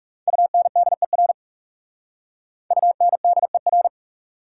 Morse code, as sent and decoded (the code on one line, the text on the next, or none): ..- -. -.. .-. / ..- -. -.. .-.
UNDER UNDER